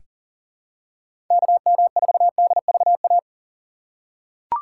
Morse code, heard as a keyed tone, about 40 wpm, 700 Hz. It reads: KM4DVA
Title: USA Call Signs - 4x Character Spacing - 40wpm